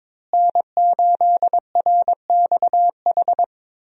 Morse code, heard as a keyed tone, about 21 wpm, 700 Hz. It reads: N8RXH
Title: USA Call Signs - 5 Characters - Repeated 3x - 22wpm